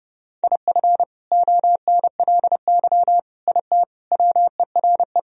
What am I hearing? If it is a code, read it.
Morse code, heard as a keyed tone, about 30 wpm, 700 Hz. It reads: IF ONLY IT WERE
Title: Sentences from Top 500 Words and Abbreviations 30wpm